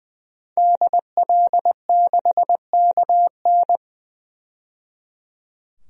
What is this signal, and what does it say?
Morse code, heard as a keyed tone, about 20 wpm, 700 Hz. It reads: DL6KN
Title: DXCC-100 Call Signs - 5 Characters - 20wpm